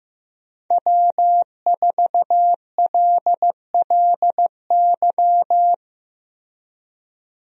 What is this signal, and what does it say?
Morse code, heard as a keyed tone, about 15 wpm, 700 Hz. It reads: W4LLY
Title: USA Call Signs - 5 Characters - Repeated 2x - 15wpm